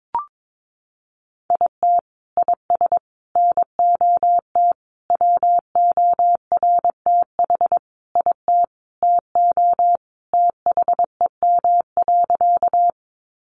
Morse code, heard as a keyed tone, about 22 wpm, 700 Hz. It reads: IT IS NOT WORTH IT TO THEM.
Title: Sentences from Top 800 Words 22wpm